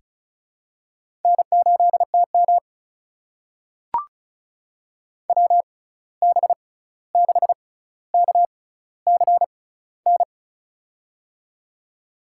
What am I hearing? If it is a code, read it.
Morse code, heard as a keyed tone, about 36 wpm, spaced as for 16 wpm, 700 Hz. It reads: N8TM WB6KCN
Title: USA Call Signs - 6x Character Spacing - 35wpm